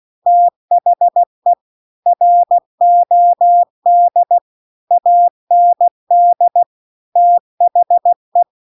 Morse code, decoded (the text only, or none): THE ROD AND THE